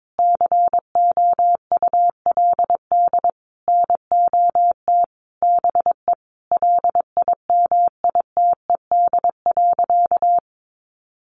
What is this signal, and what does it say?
Morse code, heard as a keyed tone, about 22 wpm, 700 Hz. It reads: COULD NOT BE LIMITED.
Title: Sentences from Top 900 Words 22wpm